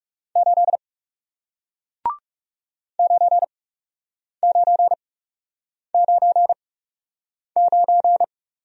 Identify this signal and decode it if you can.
Morse code, heard as a keyed tone, about 39 wpm, spaced as for 8 wpm, 700 Hz. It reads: 8 9999